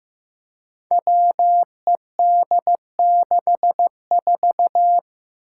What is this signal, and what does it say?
Morse code, heard as a keyed tone, about 15 wpm, 700 Hz. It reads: WED64
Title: Five Challenging Letter-Number Combinations 15wpm